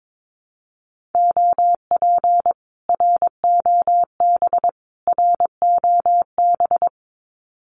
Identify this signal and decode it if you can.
Morse code, heard as a keyed tone, about 22 wpm, 700 Hz. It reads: OP ROB ROB
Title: Common QSO Phrases Encore 22wpm